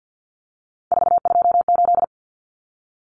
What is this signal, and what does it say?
Morse code, heard as a keyed tone, about 50 wpm, 700 Hz. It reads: 428
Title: Common Cut Numbers - 3-Digit 50wpm